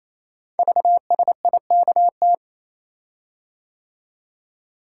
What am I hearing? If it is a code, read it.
Morse code, heard as a keyed tone, about 29 wpm, 700 Hz. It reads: VSIKT
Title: Code Groups - 5 Letters - 28wpm